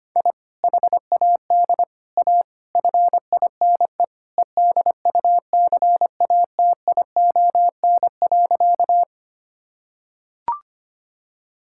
Morse code, heard as a keyed tone, about 25 wpm, 700 Hz. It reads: I HAD A FINE EDUCATION.
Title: Sentences from Top 1000 Words - Review Part 6 - 25wpm